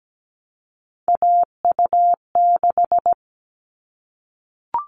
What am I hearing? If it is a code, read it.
Morse code, heard as a keyed tone, about 17 wpm, 700 Hz. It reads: AU6
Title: All Cut Numbers - 3-Digit 17wpm